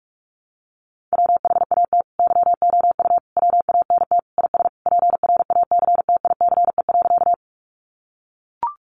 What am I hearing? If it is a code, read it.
Morse code, heard as a keyed tone, about 44 wpm, 700 Hz. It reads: WHAT YOU WANT IS PRACTICE.
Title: Sentences from Top 700 Words 45wpm